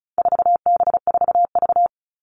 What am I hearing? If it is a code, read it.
Morse code, heard as a keyed tone, about 34 wpm, 700 Hz. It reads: 4B4V
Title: All Cut Numbers - 4-Digit 35wpm